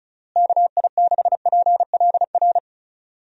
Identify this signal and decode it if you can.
Morse code, heard as a keyed tone, about 36 wpm, 700 Hz. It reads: KI6PLR
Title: USA Call Signs - 6 Characters - Repeated 2x - 35wpm